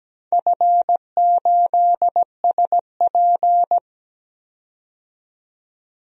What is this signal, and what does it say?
Morse code, heard as a keyed tone, about 17 wpm, 700 Hz. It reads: F8SP
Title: DXCC-100 Call Signs - 4 Characters - 17wpm